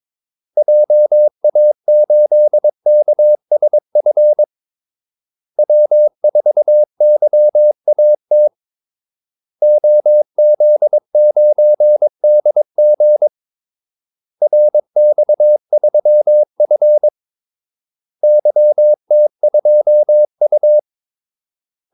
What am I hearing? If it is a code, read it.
Morse code, heard as a keyed tone, about 22 wpm, 600 Hz. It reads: JA8KSF W4YAT OZ9DG RX3F YT2U
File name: Callsigns-22wpm